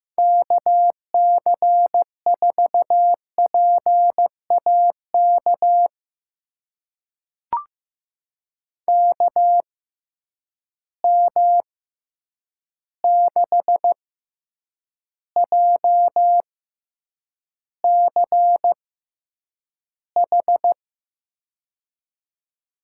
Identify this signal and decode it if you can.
Morse code, heard as a keyed tone, about 15 wpm, spaced as for 8 wpm, 700 Hz. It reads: KC4PAK KM6JCH
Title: USA Call Signs - 6 Characters - 6x Character Spacing - 15wpm